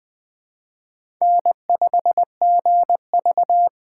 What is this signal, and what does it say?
Morse code, heard as a keyed tone, about 21 wpm, 700 Hz. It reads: N5GV
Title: USA Call Signs - 4 Characters - 3x Character Spacing - 20wpm